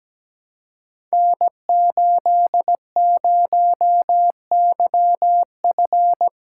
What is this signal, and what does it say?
Morse code, heard as a keyed tone, about 17 wpm, 700 Hz. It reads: N80YF
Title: Code Groups - 5 Letter:Numbers - 17wpm